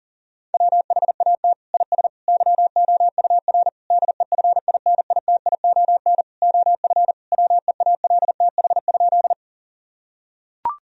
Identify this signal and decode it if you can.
Morse code, heard as a keyed tone, about 37 wpm, 700 Hz. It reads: WHAT IS YOUR DEFINITION OF WEALTH?
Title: Sentences from Top 1000 Words - Review Part 4 - 40wpm